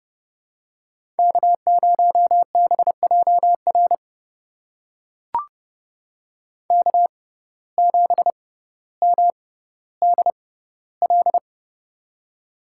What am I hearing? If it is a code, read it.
Morse code, heard as a keyed tone, about 32 wpm, spaced as for 16 wpm, 700 Hz. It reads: K0BJR K7MDL
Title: USA Call Signs - 5 Characters - 6x Character Spacing - 30wpm